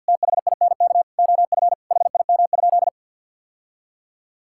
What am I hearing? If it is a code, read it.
Morse code, heard as a keyed tone, about 50 wpm, 700 Hz. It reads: THINK OF HIM?
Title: Sentences from Top 1000 Words - Review Part 5 - 50wpm